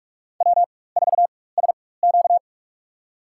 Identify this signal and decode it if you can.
Morse code, heard as a keyed tone, about 41 wpm, spaced as for 20 wpm, 700 Hz. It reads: W4SQ
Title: USA Call Signs - 4x Character Spacing - 45wpm